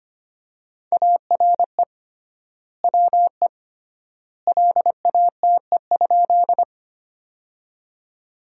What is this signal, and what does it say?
Morse code, heard as a keyed tone, about 25 wpm, 700 Hz. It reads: ARE WE LATE?
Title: 3x Word-Spacing - Sentences from Top 300 Words - 25wpm